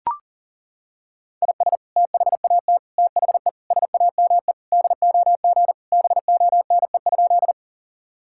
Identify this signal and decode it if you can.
Morse code, heard as a keyed tone, about 37 wpm, 700 Hz. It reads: IS THAT THE SAME DOG BONE?
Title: Sentences from Top 1600 Words 40wpm